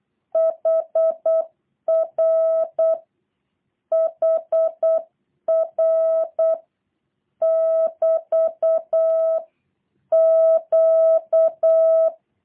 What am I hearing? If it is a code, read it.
Morse code, heard as a keyed tone, about 8 wpm, 650 Hz. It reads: HR HR <BT> Q
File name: jh8cbh / 2023cw / review1-6